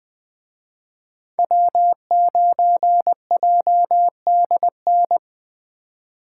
Morse code, heard as a keyed tone, about 20 wpm, 700 Hz. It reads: W9JDN
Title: Speed Racing - USA 1x3 Call Signs 17wpm